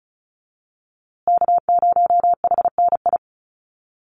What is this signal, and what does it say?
Morse code, heard as a keyed tone, about 34 wpm, 700 Hz. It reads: K0HNI
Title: USA Call Signs Repeated 3x 35wpm